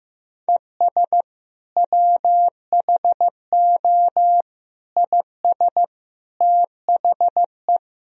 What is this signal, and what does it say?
Morse code, heard as a keyed tone, about 15 wpm, 700 Hz. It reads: ES WHO IS THE
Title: Sentences from Top 200 Words and Abbreviations 15wpm